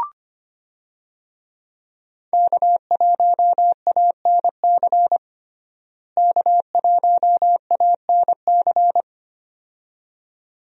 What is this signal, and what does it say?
Morse code, heard as a keyed tone, about 25 wpm, 700 Hz. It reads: K1ANC K1ANC
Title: USA Call Signs - 5 Characters - Repeated 2x - 25wpm